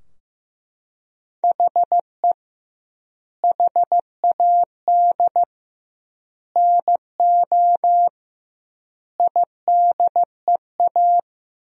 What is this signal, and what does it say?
Morse code, heard as a keyed tone, about 15 wpm, 700 Hz. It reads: HE HAD NO IDEA
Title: 2x Word-Spacing - Sentences from Top 300 Words - 15wpm